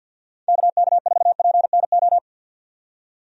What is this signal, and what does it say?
Morse code, heard as a keyed tone, about 52 wpm, 700 Hz. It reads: KK4PNO
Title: USA Call Signs - 6 Characters - 50wpm